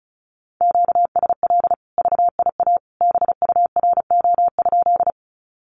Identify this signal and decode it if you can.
Morse code, heard as a keyed tone, about 34 wpm, 700 Hz. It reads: QSL VIA BURO?
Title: Common QSO Phrases 35wpm